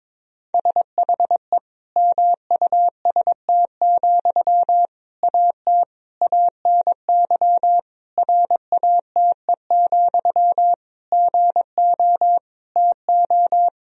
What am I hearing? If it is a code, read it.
Morse code, heard as a keyed tone, about 22 wpm, 700 Hz. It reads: SHE MUST, AT ANY RATE, GO TO
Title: Sentences from Top 1000 Words - Review Part 4 - 22wpm